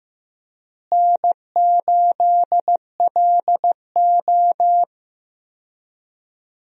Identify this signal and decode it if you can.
Morse code, heard as a keyed tone, about 15 wpm, 700 Hz. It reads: N8LO
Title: USA Call Signs - 4 Characters - Repeated 2x - 15wpm